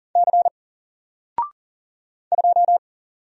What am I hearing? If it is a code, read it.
Morse code, heard as a keyed tone, about 39 wpm, 700 Hz. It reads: C 2
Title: Single Character (Letter:Number:Punctuation:Prosign) 40wpm